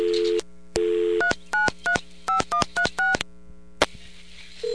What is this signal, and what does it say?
DTMF, heard as telephone touch tones, tones of about 110 ms. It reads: 3935136